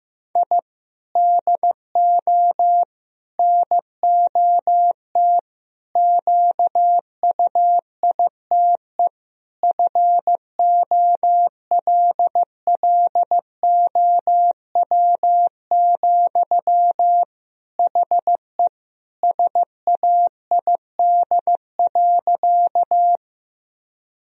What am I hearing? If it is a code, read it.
Morse code, heard as a keyed tone, about 15 wpm, 700 Hz. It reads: I DO NOT QUITE FOLLOW, HE SAID.
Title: Sentences from Top 1000 Words - Review Part 1 - 15wpm